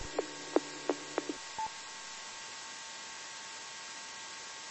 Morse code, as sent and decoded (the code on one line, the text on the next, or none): none